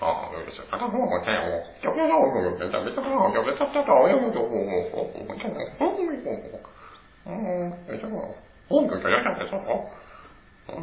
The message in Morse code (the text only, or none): none